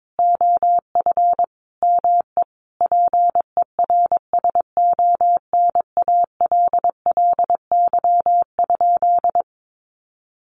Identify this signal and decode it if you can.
Morse code, heard as a keyed tone, about 22 wpm, 700 Hz. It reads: OF ME PERSONALLY?
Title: Sentences from Top 1400 Words 22wpm